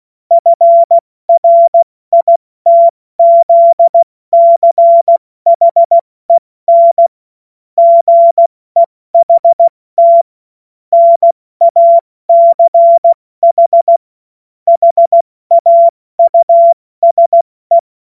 Morse code, heard as a keyed tone, about 16 wpm, 680 Hz. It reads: FRITZCHEN GEHT NACH HAUSE
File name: Morsekurs / Witze / Witz 10. August 2021, 70 F80